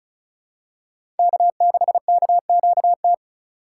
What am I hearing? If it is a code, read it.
Morse code, heard as a keyed tone, about 36 wpm, 700 Hz. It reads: K6KQT
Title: USA Call Signs - 5 Characters - 2x Character Spacing - 35wpm